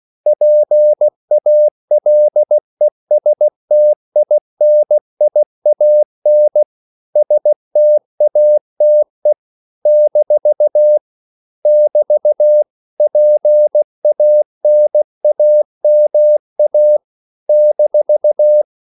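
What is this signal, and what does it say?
Morse code, heard as a keyed tone, about 16 wpm, 600 Hz. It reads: PALESTINIAN STATE - <BT> PANAMA -
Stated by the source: CW practice of national capitals at 16 words per minute, from ON6ZQ.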